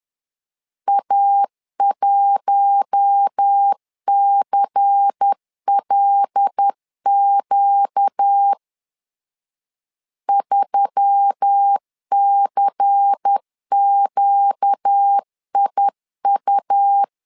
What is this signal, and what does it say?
Morse code, heard as a keyed tone, about 10 wpm, 800 Hz. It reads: A1CLQ 3CQIU
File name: LEZIONI / 5 LBVC / 5 LBVC 10wpm 2349